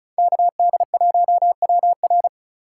Morse code, heard as a keyed tone, about 46 wpm, 700 Hz. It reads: KD1WR